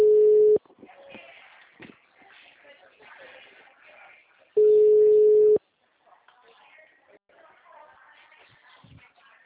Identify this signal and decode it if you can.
Ringback tone (Europe, 425 Hz), with speech.
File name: out- -2006- - - .40